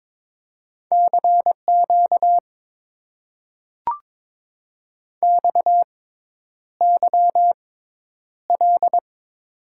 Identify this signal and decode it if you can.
Morse code, heard as a keyed tone, about 22 wpm, spaced as for 10 wpm, 700 Hz. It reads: CQ XYL
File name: Top 90 QSO Elements - 6x Character Spacing - 22wpm